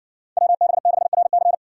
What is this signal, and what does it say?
Morse code, heard as a keyed tone, about 60 wpm, 700 Hz. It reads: WB6RC